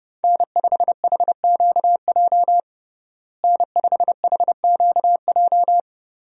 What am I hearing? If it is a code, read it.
Morse code, heard as a keyed tone, about 30 wpm, 700 Hz. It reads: N5HQJ N5HQJ
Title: USA Call Signs - 5 Characters - Repeated 3x - 30wpm